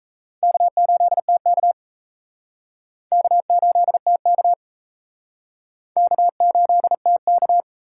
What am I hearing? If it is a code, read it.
Morse code, heard as a keyed tone, about 36 wpm, 700 Hz. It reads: K8TK K8TK K8TK